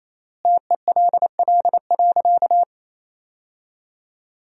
Morse code, heard as a keyed tone, about 28 wpm, 700 Hz. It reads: TELL.
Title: 4x Word-Spacing - Sentences from Top 300 Words - 28wpm